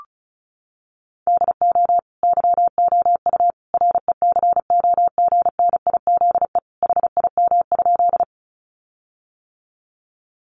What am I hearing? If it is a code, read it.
Morse code, heard as a keyed tone, about 34 wpm, 700 Hz. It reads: DO YOU RECOGNIZE HIM?
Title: Sentences from Top 1400 Words 35wpm